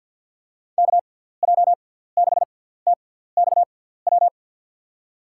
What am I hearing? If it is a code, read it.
Morse code, heard as a keyed tone, about 53 wpm, spaced as for 17 wpm, 700 Hz. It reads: KJ6TXW
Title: USA Call Signs - 6x Character Spacing - 50wpm